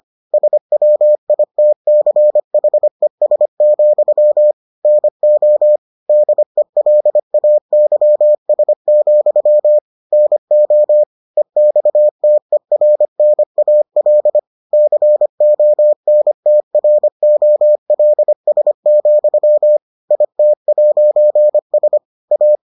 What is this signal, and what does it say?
Morse code, heard as a keyed tone, about 25 wpm, 600 Hz. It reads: SWITCHES, NO DELAYS, NO EXTERNAL CONTROLS, IT'S A